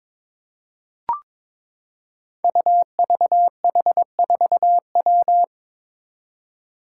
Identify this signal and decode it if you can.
Morse code, heard as a keyed tone, about 22 wpm, 700 Hz. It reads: UVH4W
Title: Five Challenging Letter-Number Combinations 22wpm